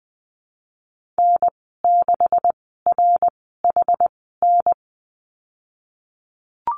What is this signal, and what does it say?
Morse code, heard as a keyed tone, about 20 wpm, spaced as for 15 wpm, 700 Hz. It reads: N6RHN
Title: 154 Lesson 25g - Letter p - Call Signs 20wpm